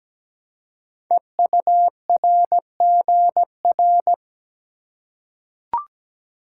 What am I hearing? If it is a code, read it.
Morse code, heard as a keyed tone, about 17 wpm, 700 Hz. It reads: EURGR
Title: Code Groups - 5 Letters - 17wpm